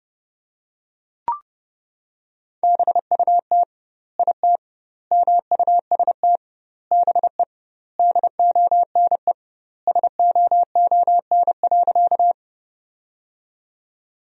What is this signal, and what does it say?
Morse code, heard as a keyed tone, about 31 wpm, 700 Hz. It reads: BUT IT MUST BE DONE SOON.
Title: 2x Word-Spacing - Sentences from Top 500 Words - 30wpm